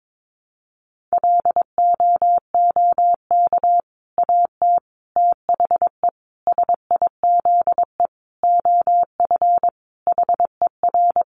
Morse code, heard as a keyed tone, about 22 wpm, 700 Hz. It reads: LOOK AT THE SIZE OF HER
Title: Sentences from Top 1200 Words 22wpm